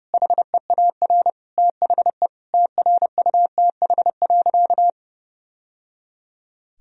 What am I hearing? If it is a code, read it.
Morse code, heard as a keyed tone, about 30 wpm, 700 Hz. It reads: HEAR THE TRUTH.
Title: Sentences from Top 1000 Words - Review Part 1 - 30wpm